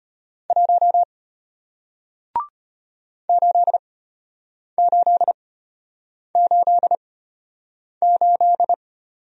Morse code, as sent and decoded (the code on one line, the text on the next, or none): .---- / ---.. ---.. ---.. ---..
1 8888